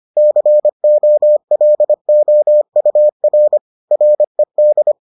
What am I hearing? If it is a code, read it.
Morse code, heard as a keyed tone, about 25 wpm, 600 Hz. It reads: COLOUR RED